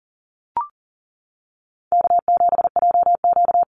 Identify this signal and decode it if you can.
Morse code, heard as a keyed tone, about 40 wpm, 700 Hz. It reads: K7JQ
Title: Easy Call Signs Repeated 3x - 40wpm